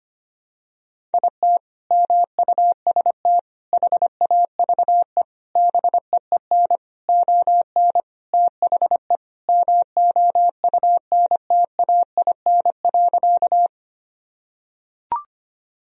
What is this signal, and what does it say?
Morse code, heard as a keyed tone, about 25 wpm, 700 Hz. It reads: IT MUST HAVE BEEN ON THE MOUNTAIN.
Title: Sentences from Top 1000 Words - Review Part 1 - 25wpm